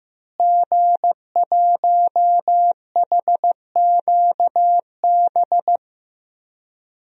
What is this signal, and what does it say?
Morse code, heard as a keyed tone, about 15 wpm, 700 Hz. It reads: G1HQB
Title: DXCC-100 Call Signs - 5 Characters - Repeated 3x - 15wpm